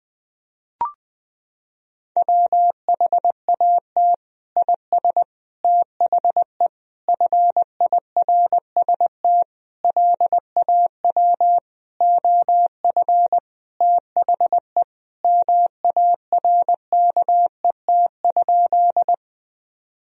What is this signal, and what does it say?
Morse code, heard as a keyed tone, about 20 wpm, 700 Hz. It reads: WHAT IS THE FIRST LAW OF THE MARKET?
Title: Sentences from Top 1000 Words - Review Part 3 - 20wpm